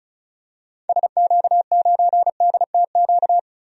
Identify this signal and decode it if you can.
Morse code, heard as a keyed tone, about 34 wpm, 700 Hz. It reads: SQ9DTQ
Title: DXCC-100 Call Signs - 6 Characters - 35wpm